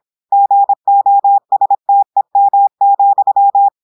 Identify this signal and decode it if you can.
Morse code, heard as a keyed tone, about 26 wpm, 800 Hz. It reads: GOSTEM,
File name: portugais texte en clair 2 vitesse 26 mots minutes